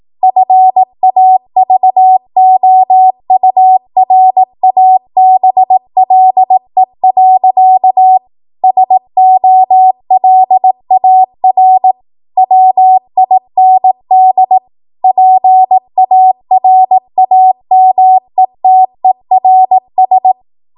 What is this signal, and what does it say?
Morse code, heard as a keyed tone, about 18 wpm, 750 Hz. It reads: FAVOURABLE. SOLAR WIND PARAMETERS
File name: Morse / 18WPM